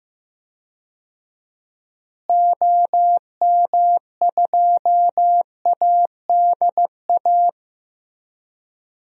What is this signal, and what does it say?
Morse code, heard as a keyed tone, about 14 wpm, 700 Hz. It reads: OM2ADA
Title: DXCC-100 Call Signs - 6 Characters - Repeated 3x - 15wpm